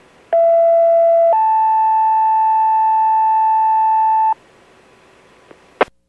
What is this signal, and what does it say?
Two-tone page, heard as a one-second tone, then a three-second tone, about 670 then 880 Hz. It is heard from 0.3 to 4.3 s.